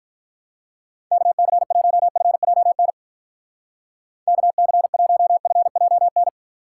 Morse code, heard as a keyed tone, about 45 wpm, 700 Hz. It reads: KC1FJN KC1FJN